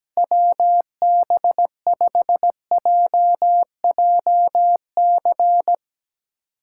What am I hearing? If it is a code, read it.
Morse code, heard as a keyed tone, about 17 wpm, 700 Hz. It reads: WB5JJC
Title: USA Callsign DE Callsign 17wpm